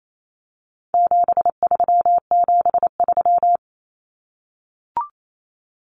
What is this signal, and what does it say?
Morse code, heard as a keyed tone, about 28 wpm, 700 Hz. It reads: 7373
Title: Common Cut Numbers - 4-Digit 28wpm